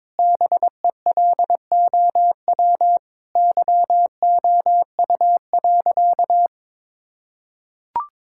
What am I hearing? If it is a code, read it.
Morse code, heard as a keyed tone, about 22 wpm, 700 Hz. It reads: BELOW YOU.
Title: Sentences from Top 1000 Words - Review Part 5 - 22wpm